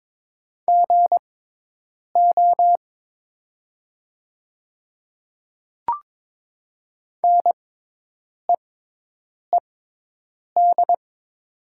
Morse code, heard as a keyed tone, about 22 wpm, spaced as for 8 wpm, 700 Hz. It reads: GO NEED
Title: 6x Character Spacing - Top 300 Words - 22wpm